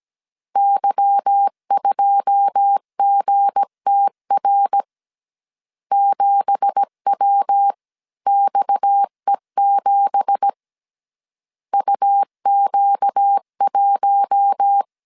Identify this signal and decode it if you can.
Morse code, heard as a keyed tone, about 17 wpm, 800 Hz. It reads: Y2GTR 7WXE7 UQ1
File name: CW / DET / 1 9 1500C S C 16wpm 2909